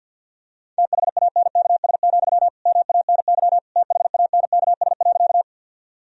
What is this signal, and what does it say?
Morse code, heard as a keyed tone, about 46 wpm, 700 Hz. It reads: THANKS, MANY THANKS.
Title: Sentences from Top 1000 Words - Review Part 3 - 50wpm